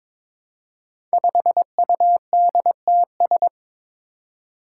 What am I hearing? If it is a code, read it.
Morse code, heard as a keyed tone, about 22 wpm, 700 Hz. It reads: 5UDTS
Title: Five Challenging Letter-Number Combinations 22wpm